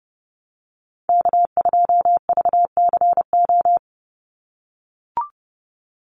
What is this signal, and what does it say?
Morse code, heard as a keyed tone, about 30 wpm, 700 Hz. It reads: K2VCO
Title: Call Signs - 30wpm